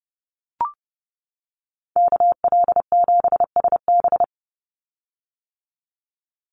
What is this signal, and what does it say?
Morse code, heard as a keyed tone, about 30 wpm, 700 Hz. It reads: KL7SB
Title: Call Signs - 30wpm